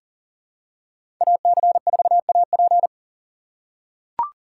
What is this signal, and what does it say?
Morse code, heard as a keyed tone, about 40 wpm, 700 Hz. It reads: AC4AP
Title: USA Call Signs - 5 Characters - 2x Character Spacing - 40wpm